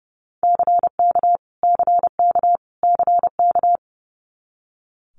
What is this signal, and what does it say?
Morse code, heard as a keyed tone, about 30 wpm, 700 Hz. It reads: CK CK CK
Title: Multi-letter Phonemes Repeated 3x 30wpm